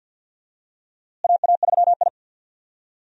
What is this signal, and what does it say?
Morse code, heard as a keyed tone, about 51 wpm, 700 Hz. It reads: AA3I
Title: USA Call Signs - 4 Characters - Repeated 2x - 50wpm